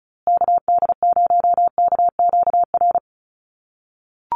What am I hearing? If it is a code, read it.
Morse code, heard as a keyed tone, about 34 wpm, 700 Hz. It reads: KD0KQR
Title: USA Call Signs Repeated 2x 35wpm